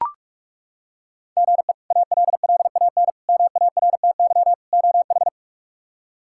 Morse code, heard as a keyed tone, about 42 wpm, 700 Hz. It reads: GE ALLAN MARTY OH
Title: K1USN SST QSO Practice - 45wpm